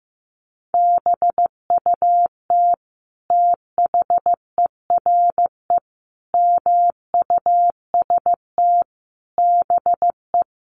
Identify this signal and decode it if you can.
Morse code, heard as a keyed tone, about 15 wpm, 700 Hz. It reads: BUT THERE MUST BE
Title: Sentences from Top 1500 Words 15wpm